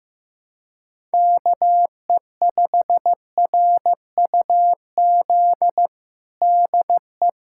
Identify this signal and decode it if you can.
Morse code, heard as a keyed tone, about 16 wpm, 700 Hz. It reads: KE5RUZ DE
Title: USA Callsign DE Callsign - 6 Characters - 15wpm